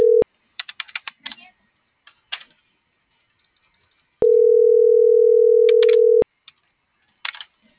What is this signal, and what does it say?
Ringback tone (North America), 440 + 480 Hz.